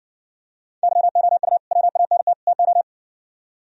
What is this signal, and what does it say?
Morse code, heard as a keyed tone, about 60 wpm, 700 Hz. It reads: YOU WANT TO